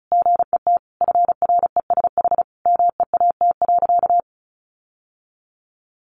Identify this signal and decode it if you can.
Morse code, heard as a keyed tone, about 34 wpm, 700 Hz. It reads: GET FRESH MEAT.
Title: Sentences from Top 1200 Words 35wpm